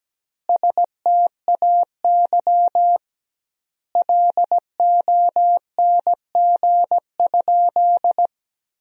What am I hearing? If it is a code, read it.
Morse code, heard as a keyed tone, about 17 wpm, 700 Hz. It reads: STAY LONG?
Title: 2x Word-Spacing - Sentences from Top 500 Words - 17wpm